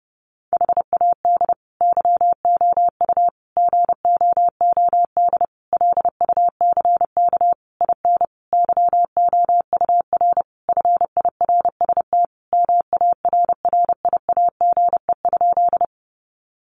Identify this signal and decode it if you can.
Morse code, heard as a keyed tone, about 30 wpm, 700 Hz. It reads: HAD YOU GOOD LUCK IN YOUR FIRST MARRIAGE?
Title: Sentences from Top 1500 Words 30wpm